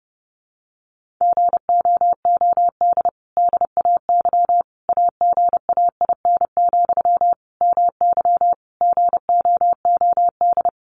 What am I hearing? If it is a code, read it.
Morse code, heard as a keyed tone, about 30 wpm, 700 Hz. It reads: GOOD DAY AGAIN, MY GOOD
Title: Sentences from Top 1100 Words 30wpm